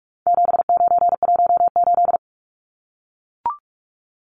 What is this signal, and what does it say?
Morse code, heard as a keyed tone, about 44 wpm, 700 Hz. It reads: 7918
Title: Numbers - 4-Digit 45wpm